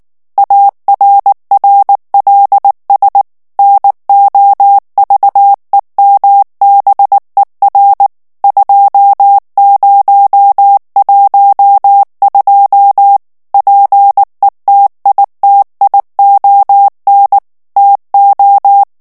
Morse code, heard as a keyed tone, about 19 wpm, 800 Hz. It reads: ARRLS NOVEMBER 2012 PETITION TO
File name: ANGLAIS texte clair 3 vitesse 18 mots minute